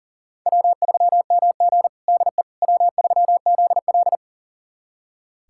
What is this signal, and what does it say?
Morse code, heard as a keyed tone, about 40 wpm, 700 Hz. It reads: W3MG DE W3ZL
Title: USA Callsign DE Callsign - 4 Characters - 40wpm